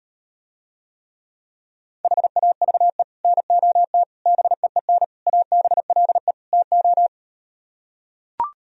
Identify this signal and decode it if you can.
Morse code, heard as a keyed tone, about 38 wpm, 700 Hz. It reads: HAVE NOT BEEN ABLE TO